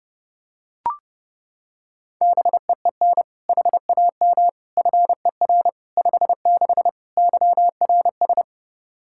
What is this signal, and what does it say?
Morse code, heard as a keyed tone, about 32 wpm, 700 Hz. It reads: BEEN HAM FER 56 YRS
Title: Practice QSO - Exchange 3 - More Info - 30wpm